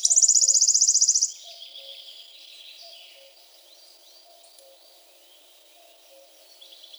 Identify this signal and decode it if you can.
There is no code.